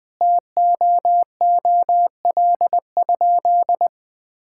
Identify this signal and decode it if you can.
Morse code, heard as a keyed tone, about 20 wpm, 700 Hz. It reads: TOOL?
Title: Sentences from Top 1000 Words - Review Part 6 - 20wpm